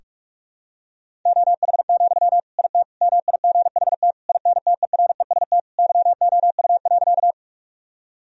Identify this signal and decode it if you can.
Morse code, heard as a keyed tone, about 42 wpm, 700 Hz. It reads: OH, IT MIGHT INTEREST YOU.
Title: Sentences from Top 1000 Words - Review Part 2 - 45wpm